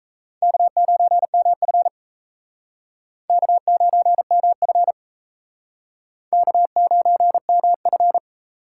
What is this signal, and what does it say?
Morse code, heard as a keyed tone, about 37 wpm, 700 Hz. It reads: K9MF K9MF K9MF